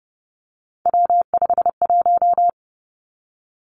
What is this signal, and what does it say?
Morse code, heard as a keyed tone, about 30 wpm, 700 Hz. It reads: W51
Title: Three Challenging Letter-Number Combinations 30wpm